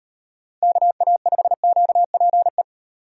Morse code, heard as a keyed tone, about 36 wpm, 700 Hz. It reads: KA5QPE